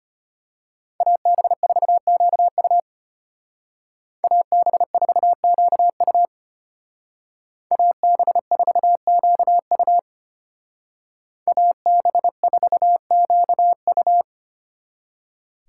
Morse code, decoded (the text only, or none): AB4QU AB4QU AB4QU AB4QU